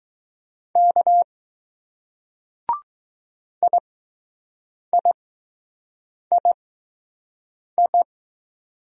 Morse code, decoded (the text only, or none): K IIII